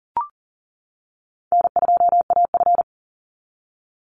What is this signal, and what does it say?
Morse code, heard as a keyed tone, about 40 wpm, 700 Hz. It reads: N2AF
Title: NA SOTA Chaser Callsigns 40wpm